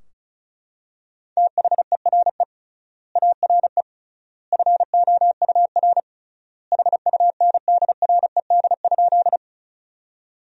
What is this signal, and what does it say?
Morse code, heard as a keyed tone, about 36 wpm, 700 Hz. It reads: THERE ARE FOUR HUNDRED?
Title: 3x Word-Spacing - Sentences from Top 400 Words - 35wpm